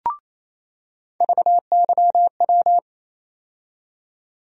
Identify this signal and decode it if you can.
Morse code, heard as a keyed tone, about 28 wpm, 700 Hz. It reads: VYW
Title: USA Call Sign Postfixes - 3 Letters 28wpm